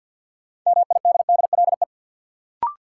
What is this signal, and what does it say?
Morse code, heard as a keyed tone, about 53 wpm, 700 Hz. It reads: MIDDLE
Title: Sets of 2 Words - New Word Drill - 50wpm